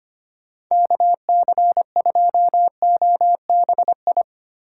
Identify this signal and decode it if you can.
Morse code, heard as a keyed tone, about 31 wpm, 700 Hz. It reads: KC2OBI
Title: Speed Racing - USA 2x3 Call Signs 25wpm